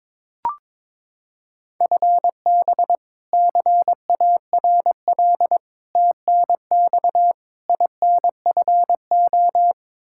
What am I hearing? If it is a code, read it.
Morse code, heard as a keyed tone, about 22 wpm, 700 Hz. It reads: FB CARL TNX INFO
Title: Practice QSO - Exchange 3 - More Info - 22wpm